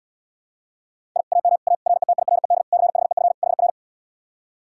Morse code, heard as a keyed tone, about 80 wpm, 700 Hz. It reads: I AM A RETIRED CHEF HR